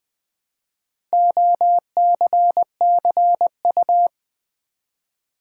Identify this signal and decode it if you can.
Morse code, heard as a keyed tone, about 20 wpm, 700 Hz. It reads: OCCU
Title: Build A Word - 1 Letter At A Time - 3 to 5-Letter Words - 20wpm